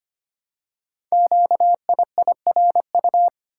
Morse code, heard as a keyed tone, about 25 wpm, 700 Hz. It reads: QIIRU
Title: Code Groups - 5 Letters - 25wpm